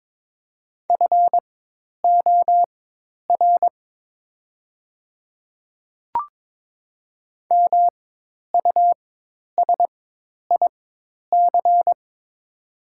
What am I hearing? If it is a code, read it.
Morse code, heard as a keyed tone, about 23 wpm, spaced as for 10 wpm, 700 Hz. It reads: FOR MUSIC
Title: 4x Character Spacing - Top 300 Words - 22wpm